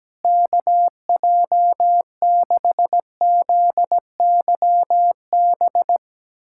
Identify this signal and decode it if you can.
Morse code, heard as a keyed tone, about 17 wpm, 700 Hz. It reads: KJ6ZYB
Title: USA Callsign DE Callsign - 6 Characters - 17wpm